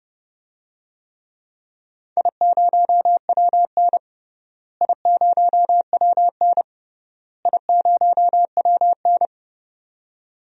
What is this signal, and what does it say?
Morse code, heard as a keyed tone, about 30 wpm, 700 Hz. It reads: I0WN I0WN I0WN
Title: DXCC-100 Call Signs - 4 Characters - Repeated 3x - 30wpm